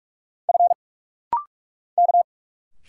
Morse code, heard as a keyed tone, about 44 wpm, 700 Hz. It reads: F K
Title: Single Letters - Rapid-Fire 45wpm